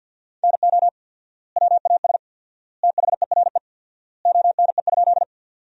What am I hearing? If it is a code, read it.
Morse code, heard as a keyed tone, about 50 wpm, 700 Hz. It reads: NO WAS THERE ONE?
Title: 4x Word-Spacing - Sentences from Top 100 Words - 50wpm